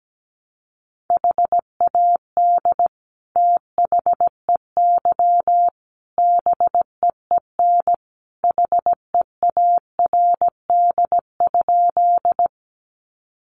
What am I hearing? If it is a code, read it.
Morse code, heard as a keyed tone, about 17 wpm, 700 Hz. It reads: HAD THEY BEEN HEARD?
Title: Sentences from Top 400 Words 17wpm